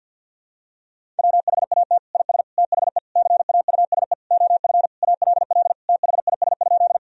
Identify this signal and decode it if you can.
Morse code, heard as a keyed tone, about 52 wpm, 700 Hz. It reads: WHAT IS THE CAUSE OF ALL THIS?
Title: Sentences from Top 1000 Words - Review Part 1 - 50wpm